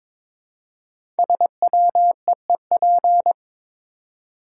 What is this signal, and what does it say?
Morse code, heard as a keyed tone, about 22 wpm, 700 Hz. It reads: SWEEP
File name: 5 Letter Words - First 3 Letters Given - 22wpm